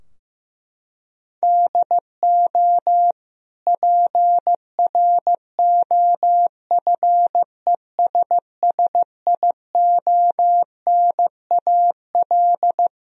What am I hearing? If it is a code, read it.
Morse code, heard as a keyed tone, about 15 wpm, 700 Hz. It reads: DO PROFESSIONAL
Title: Sentences from Top 1000 Words - Review Part 4 - 15wpm